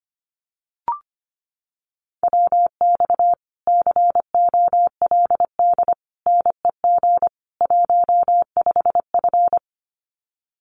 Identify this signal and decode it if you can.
Morse code, heard as a keyed tone, about 25 wpm, 700 Hz. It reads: WX COLD NEG 15F
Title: Common QSO Phrases Encore 25wpm